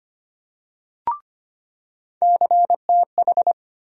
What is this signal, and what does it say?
Morse code, heard as a keyed tone, about 25 wpm, 700 Hz. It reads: CTH
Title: USA Call Sign Postfixes - 3 Letters 25wpm